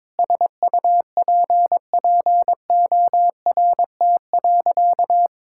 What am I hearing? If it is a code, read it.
Morse code, heard as a keyed tone, about 22 wpm, 700 Hz. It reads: SUPPORT.
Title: Sentences from Top 1000 Words - Review Part 5 - 22wpm